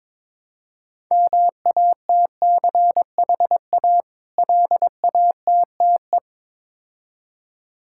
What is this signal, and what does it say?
Morse code, heard as a keyed tone, about 21 wpm, 700 Hz. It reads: MATCHA LATTE
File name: Difficult Words with Hints - 22wpm